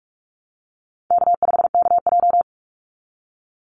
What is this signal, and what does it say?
Morse code, heard as a keyed tone, about 44 wpm, 700 Hz. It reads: K5KJ
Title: Call Signs Repeated 2x - 45wpm